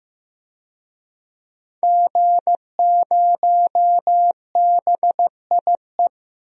Morse code, heard as a keyed tone, about 15 wpm, 700 Hz. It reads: G0BIE
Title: DXCC-100 Call Signs - 5 Characters - Repeated 3x - 15wpm